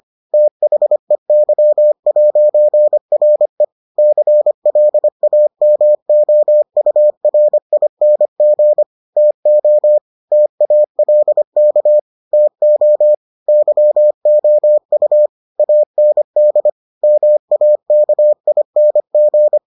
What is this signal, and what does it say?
Morse code, heard as a keyed tone, about 25 wpm, 600 Hz. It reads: THEY'RE CLAMOURING TO TALK TO YOU AND MAKING